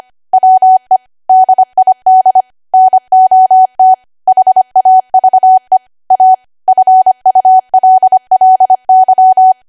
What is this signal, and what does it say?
Morse code, heard as a keyed tone, about 25 wpm, 750 Hz. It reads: WE DID NOT HAVE A FULLY
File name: Archive / 25 WPM / 25WPM